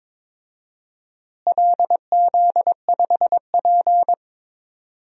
Morse code, heard as a keyed tone, about 22 wpm, 700 Hz. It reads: LZ5P
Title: DXCC-100 Call Signs - 4 Characters - Repeated 2x - 22wpm